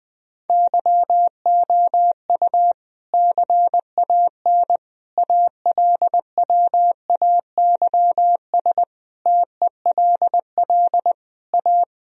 Morse code, heard as a keyed tone, about 21 wpm, 700 Hz. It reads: YOU CAN ALWAYS TELL A
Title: Sentences from Top 1000 Words - Review Part 4 - 20wpm